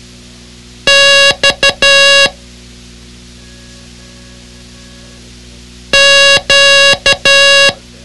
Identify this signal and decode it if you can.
Morse code, heard as a keyed tone, about 12 wpm, 1650 Hz. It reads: X Q
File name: ham radio / CodeQuick / Code Quick 06